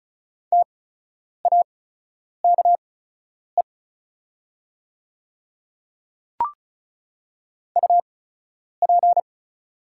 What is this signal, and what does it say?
Morse code, heard as a keyed tone, about 34 wpm, spaced as for 10 wpm, 700 Hz. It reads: TAKE UP
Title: 8x Character Spacing - Top 200 Words - 35wpm